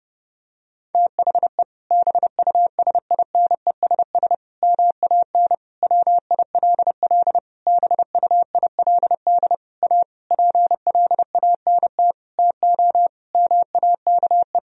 Morse code, heard as a keyed tone, about 30 wpm, 700 Hz. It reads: THE BUSINESS MAN WILL BUILD A PLANT TO MAKE
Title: Sentences from Top 1000 Words - Review Part 1 - 30wpm